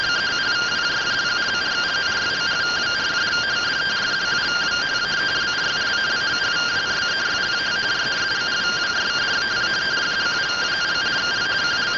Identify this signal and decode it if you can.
Two-tone teleprinter signal, about 170 Hz shift.